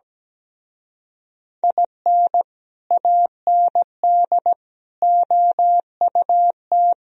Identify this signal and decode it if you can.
Morse code, heard as a keyed tone, about 17 wpm, 700 Hz. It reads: IN AND OUT
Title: Speed-Racing - Sets of 5 Words 17wpm